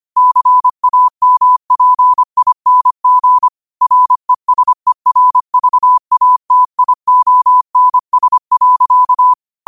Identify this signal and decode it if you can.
Morse code, heard as a keyed tone, about 25 wpm, 1000 Hz. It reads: CAMPING RESERVATIONS.
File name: qotdfiles / 25wpm